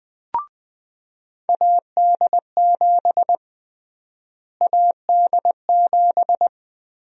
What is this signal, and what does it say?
Morse code, heard as a keyed tone, about 21 wpm, 700 Hz. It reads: AD7 AD7
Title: USA Call Sign Prefixes - Repeated 2x - 20wpm